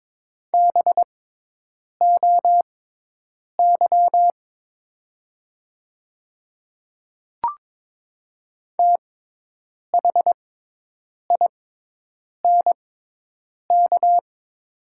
Morse code, heard as a keyed tone, about 22 wpm, spaced as for 8 wpm, 690 Hz. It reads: BOY THINK
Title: 6x Character Spacing - Top 200 Words - 22wpm